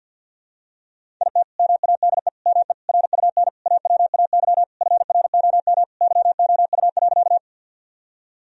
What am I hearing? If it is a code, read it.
Morse code, heard as a keyed tone, about 53 wpm, 700 Hz. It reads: IT MADE ME RUN AWAY FROM YOU.
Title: Sentences from Top 1000 Words - Review Part 1 - 50wpm